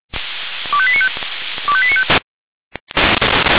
Five-tone selective call (ZVEI), heard as quick runs of five tones, tones of about 70 ms. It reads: 25895 25895
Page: ZVEI